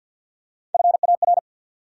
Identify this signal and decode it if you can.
Morse code, heard as a keyed tone, about 53 wpm, 700 Hz. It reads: FAR
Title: Three Letter Words - 50wpm